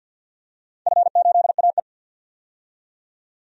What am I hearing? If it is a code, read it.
Morse code, heard as a keyed tone, about 47 wpm, 700 Hz. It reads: F8AE
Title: DXCC-100 Call Signs - 4 Characters - 50wpm